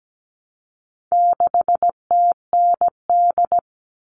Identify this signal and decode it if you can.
Morse code, heard as a keyed tone, about 17 wpm, 700 Hz. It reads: 6TND
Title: All Cut Numbers - 4-Digit 17wpm